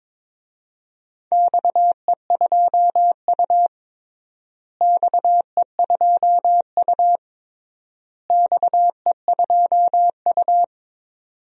Morse code, heard as a keyed tone, about 22 wpm, 700 Hz. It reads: XE2U XE2U XE2U
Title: DXCC-100 Call Signs - 4 Characters - Repeated 3x - 22wpm